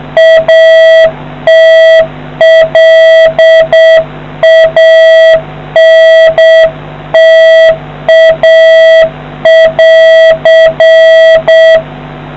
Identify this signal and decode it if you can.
Morse code, heard as a keyed tone, about 7 wpm, 650 Hz. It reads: ATLANTA<AR>